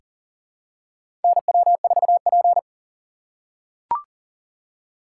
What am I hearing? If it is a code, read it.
Morse code, heard as a keyed tone, about 40 wpm, 700 Hz. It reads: NW4P
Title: USA Call Signs - 4 Characters - 2x Character Spacing - 40wpm